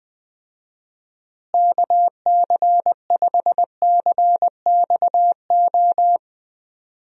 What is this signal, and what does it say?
Morse code, heard as a keyed tone, about 20 wpm, 700 Hz. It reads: KC5CXO
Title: USA Call Signs - 6 Characters - Repeated 2x - 20wpm